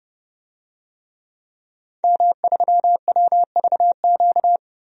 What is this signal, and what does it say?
Morse code, heard as a keyed tone, about 32 wpm, 700 Hz. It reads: M3WVQ
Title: DXCC-100 Call Signs - 5 Characters - Repeated 2x - 30wpm